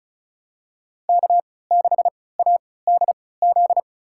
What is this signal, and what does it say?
Morse code, heard as a keyed tone, about 36 wpm, spaced as for 20 wpm, 690 Hz. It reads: K6ADZ
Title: USA Call Signs - 5 Characters - 3x Character Spacing - 35wpm